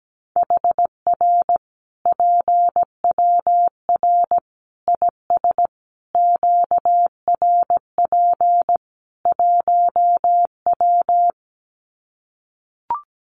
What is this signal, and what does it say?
Morse code, heard as a keyed tone, about 17 wpm, 700 Hz. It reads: HR PWR IS QRP 1W
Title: Common QSO Phrases 17wpm